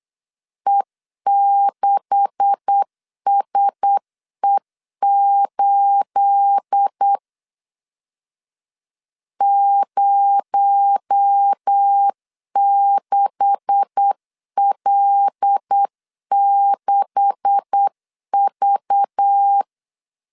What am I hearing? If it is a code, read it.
Morse code, heard as a keyed tone, about 8 wpm, 800 Hz. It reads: E6SE8 06L6V
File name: LEZIONI / 9 286 8wpm 3425